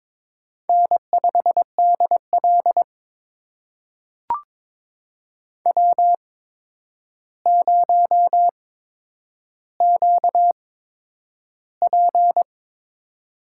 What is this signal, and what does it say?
Morse code, heard as a keyed tone, about 22 wpm, spaced as for 9 wpm, 700 Hz. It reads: N5DL W0QP
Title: USA Call Signs - 4 Characters - 8x Character Spacing - 22wpm